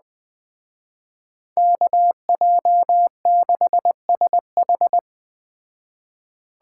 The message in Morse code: KJ6SH

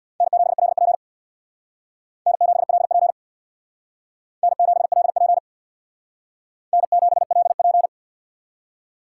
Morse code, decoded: N7LP N7LP N7LP N7LP

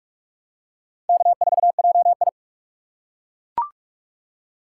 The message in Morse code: K4JI

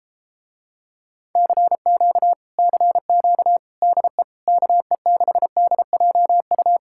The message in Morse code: CQ CQ DE KE6DJU

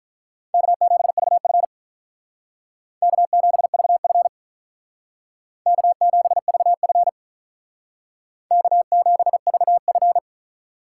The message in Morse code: K7VF K7VF K7VF K7VF

